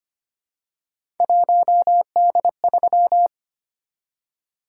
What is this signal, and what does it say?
Morse code, heard as a keyed tone, about 25 wpm, 700 Hz. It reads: 1D3
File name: Three Letter-Number Combinations - 25wpm